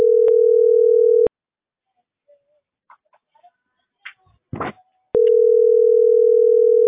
Ringback tone (North America), 440 + 480 Hz.